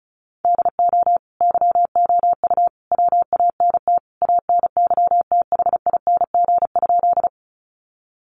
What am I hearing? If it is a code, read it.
Morse code, heard as a keyed tone, about 34 wpm, 700 Hz. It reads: DO YOU WANT ANYTHING?
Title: Sentences from Top 1000 Words 35wpm